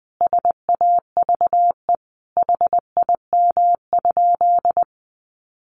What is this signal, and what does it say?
Morse code, heard as a keyed tone, about 19 wpm, 700 Hz. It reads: SAVE HIM?
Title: Sentences from Top 1100 Words 20wpm